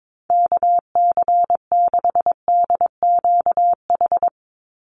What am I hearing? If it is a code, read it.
Morse code, heard as a keyed tone, about 22 wpm, 700 Hz. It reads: KC6DQH
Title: USA Call Signs 22wpm